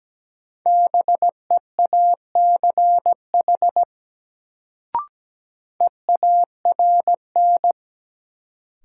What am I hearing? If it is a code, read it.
Morse code, heard as a keyed tone, about 17 wpm, 700 Hz. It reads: BEACH EARN
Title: Sentences from Top 1500 Words - New Word Drill Alt - 17wpm